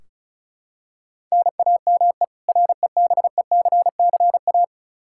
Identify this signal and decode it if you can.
Morse code, heard as a keyed tone, about 36 wpm, 700 Hz. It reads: NAME REBECCA
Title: K1USN SST QSO Practice 35wpm